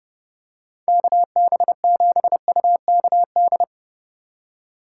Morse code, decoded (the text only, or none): KB7UKD